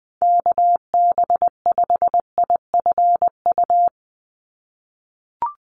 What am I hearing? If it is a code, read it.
Morse code, heard as a keyed tone, about 20 wpm, 700 Hz. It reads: KB5IFU
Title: USA Call Signs 20wpm